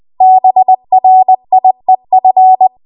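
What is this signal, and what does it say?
Morse code, heard as a keyed tone, about 20 wpm, 750 Hz. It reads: BRIEF
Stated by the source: Morse code practice at 20 WPM.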